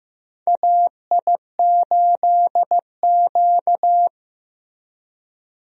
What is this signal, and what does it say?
Morse code, heard as a keyed tone, about 15 wpm, 700 Hz. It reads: AI8Q
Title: USA Call Signs - 4 Characters - Repeated 3x - 15wpm